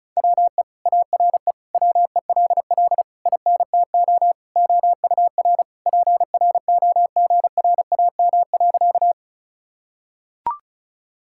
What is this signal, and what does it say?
Morse code, heard as a keyed tone, about 34 wpm, 700 Hz. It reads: WE ARE WELL INTO OUR PROGRAM.
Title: Sentences from Top 1000 Words - Review Part 5 - 35wpm